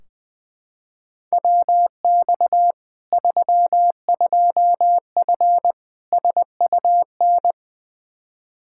Morse code, heard as a keyed tone, about 20 wpm, 700 Hz. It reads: WX 32F SUN
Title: Practice QSO - Exchange 2 - Info - 20wpm